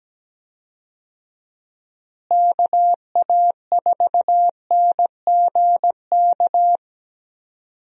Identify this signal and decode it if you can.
Morse code, heard as a keyed tone, about 17 wpm, 700 Hz. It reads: KA4NGK
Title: USA Call Signs - 6 Characters - Repeated 3x - 17wpm